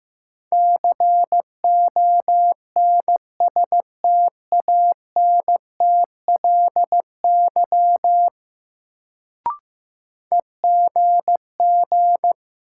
Morse code, heard as a keyed tone, about 15 wpm, 700 Hz. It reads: CONSTANTLY EGG
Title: Sentences from Top 1000 Words - Review Part 7 - New Word Drill - 15wpm